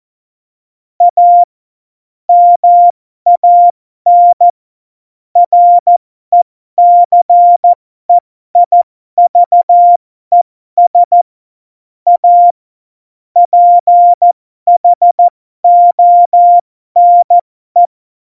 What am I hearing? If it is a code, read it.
Morse code, heard as a keyed tone, about 14 wpm, spaced as for 12 wpm, 700 Hz. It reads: A MAN RECEIVES A PHONE
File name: Morsekurs / Witze